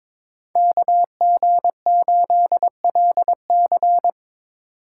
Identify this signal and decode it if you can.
Morse code, heard as a keyed tone, about 24 wpm, 700 Hz. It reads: KG8LC
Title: Speed Racing - USA 2x2 Call Signs 22wpm